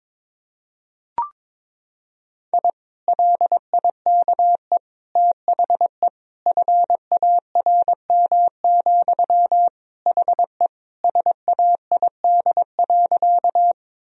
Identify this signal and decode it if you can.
Morse code, heard as a keyed tone, about 22 wpm, 700 Hz. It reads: I LIKE THE FARM, HE SAID.
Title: Sentences from Top 1000 Words - Review Part 2 - 22wpm